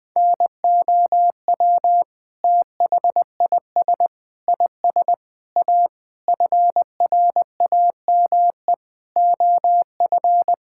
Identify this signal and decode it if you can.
Morse code, heard as a keyed tone, about 20 wpm, 700 Hz. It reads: NOW THIS IS A FRAME OF
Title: Sentences from Top 1000 Words - Review Part 6 - 20wpm